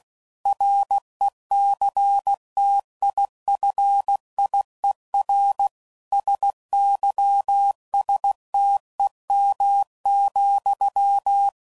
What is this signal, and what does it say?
Morse code, heard as a keyed tone, about 16 wpm, 800 Hz. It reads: RECTIFIER SYSTEM,
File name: CW Code Groups @15WPM / Technical Subjects / PwrSupply